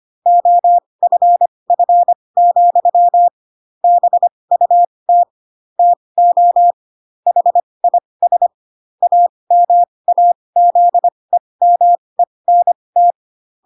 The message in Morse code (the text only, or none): OFF, BUT TO HIS AMAZEMENT